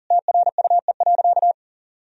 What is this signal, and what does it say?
Morse code, heard as a keyed tone, about 40 wpm, 700 Hz. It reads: TRUE.
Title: Sentences from Top 400 Words and Abbreviations 40wpm